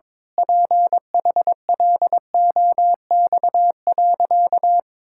Morse code, heard as a keyed tone, about 21 wpm, 700 Hz. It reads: PHLOX.
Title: Pangram Challenge v2 - 22wpm